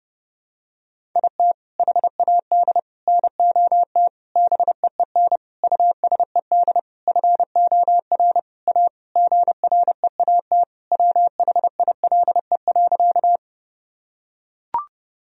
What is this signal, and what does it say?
Morse code, heard as a keyed tone, about 30 wpm, 700 Hz. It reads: IT HAD NOT BEEN USED FOR A GREAT WHILE.
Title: Sentences from Top 1000 Words - Review Part 1 - 30wpm